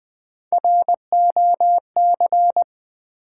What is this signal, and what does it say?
Morse code, heard as a keyed tone, about 20 wpm, 700 Hz. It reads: ROC
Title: Build A Word - 1 Letter At A Time - 4-Letter Words - 20wpm